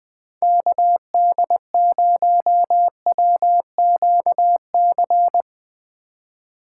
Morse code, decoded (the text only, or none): KD0WQC